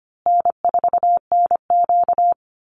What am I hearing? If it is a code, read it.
Morse code, heard as a keyed tone, about 25 wpm, 700 Hz. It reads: N4NQ
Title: Easy Call Signs Repeated 3x - 25wpm